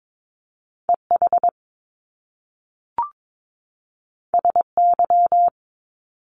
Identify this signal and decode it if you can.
Morse code, heard as a keyed tone, about 22 wpm, 700 Hz. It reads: EH SY
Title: Top Two Letter Combinations in Words 22wpm